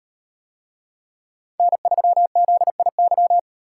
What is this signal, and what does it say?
Morse code, heard as a keyed tone, about 37 wpm, 700 Hz. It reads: N3ZIY